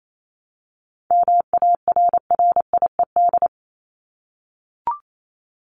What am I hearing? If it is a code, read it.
Morse code, heard as a keyed tone, about 28 wpm, 700 Hz. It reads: MARRIED
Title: ED Suffix - Top 100 Words with Base Words 28wpm